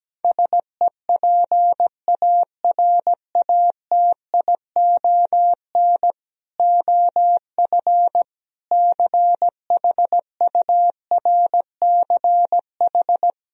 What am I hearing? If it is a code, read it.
Morse code, heard as a keyed tone, about 17 wpm, 700 Hz. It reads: SEPARATION OF CHURCH
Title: Speed-Racing - Sets of 5 Words 17wpm